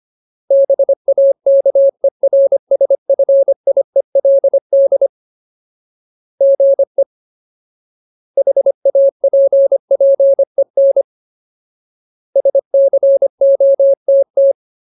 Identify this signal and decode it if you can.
Morse code, heard as a keyed tone, about 25 wpm, 550 Hz. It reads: BAKERSFIELD GE HAPPEN SCOTT
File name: Ultimate Word List / 25WPM Ultimate